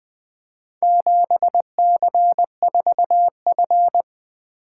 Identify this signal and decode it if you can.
Morse code, heard as a keyed tone, about 23 wpm, 700 Hz. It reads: 7C4F
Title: Speed-Racing - Four Letter-Number Combinations - 20wpm